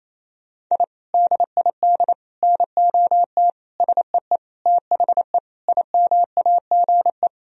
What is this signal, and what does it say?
Morse code, heard as a keyed tone, about 28 wpm, 700 Hz. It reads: I DID NOT SEE THE IMAGE
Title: Sentences from Top 1000 Words - Review Part 4 - 28wpm